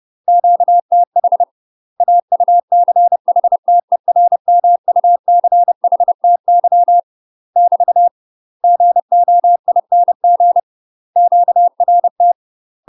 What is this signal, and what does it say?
Morse code, heard as a keyed tone, about 30 wpm, 700 Hz. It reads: QTH AUCHTERMUCHTY <BT> GOING QRT
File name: UK MASTER QSOs TRAINING/30WPM